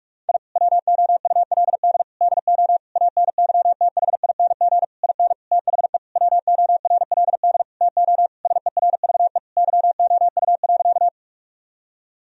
Morse code, heard as a keyed tone, about 46 wpm, 700 Hz. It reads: I WOULD DO ANYTHING IN THE WORLD TO SERVE YOU.
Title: Sentences from Top 1000 Words - Review Part 2 - 45wpm